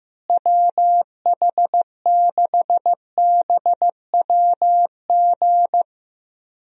Morse code, heard as a keyed tone, about 15 wpm, 700 Hz. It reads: WH6BWG
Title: USA Call Signs - 6 Characters - Repeated 3x - 15wpm